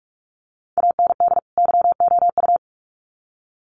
Morse code, heard as a keyed tone, about 44 wpm, 700 Hz. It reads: AND YOU
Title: Sets of 2 Words 45wpm